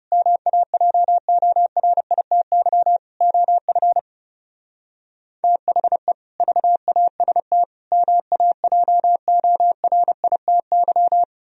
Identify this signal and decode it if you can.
Morse code, heard as a keyed tone, about 36 wpm, 700 Hz. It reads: MAJORITY OF THE VAST MAJORITY
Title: Speed-Racing - Sets of 4 Words 30wpm